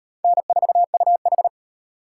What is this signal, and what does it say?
Morse code, heard as a keyed tone, about 36 wpm, 700 Hz. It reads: N4UH